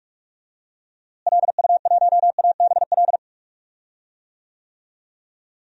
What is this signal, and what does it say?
Morse code, heard as a keyed tone, about 46 wpm, 700 Hz. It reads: LU1ABL
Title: DXCC-100 Call Signs - 6 Characters - Repeated 3x - 45wpm